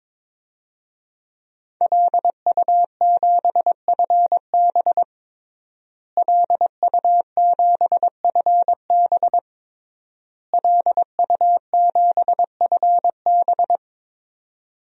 Morse code, heard as a keyed tone, about 22 wpm, 700 Hz. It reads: LU7FB LU7FB LU7FB
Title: DXCC-100 Call Signs - 5 Characters - Repeated 3x - 22wpm